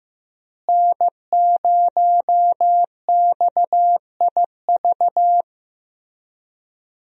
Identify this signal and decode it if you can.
Morse code, heard as a keyed tone, about 15 wpm, 700 Hz. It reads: N0XIV
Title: USA Call Signs - 5 Characters - Repeated 3x - 15wpm